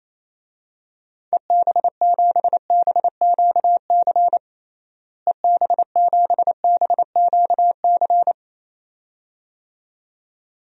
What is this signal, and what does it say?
Morse code, heard as a keyed tone, about 28 wpm, 700 Hz. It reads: EB7BQC EB7BQC
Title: DXCC-100 Call Signs - 6 Characters - Repeated 2x - 28wpm